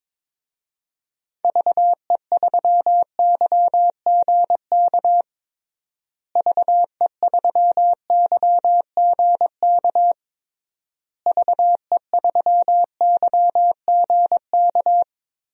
Morse code, heard as a keyed tone, about 22 wpm, 700 Hz. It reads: VE3YGK VE3YGK VE3YGK
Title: DXCC-100 Call Signs - 6 Characters - Repeated 3x - 22wpm